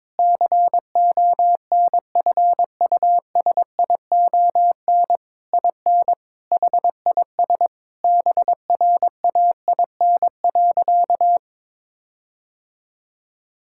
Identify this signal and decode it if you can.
Morse code, heard as a keyed tone, about 22 wpm, 700 Hz. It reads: CONFUSION IN HIS BRAIN.
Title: Sentences from Top 1600 Words 22wpm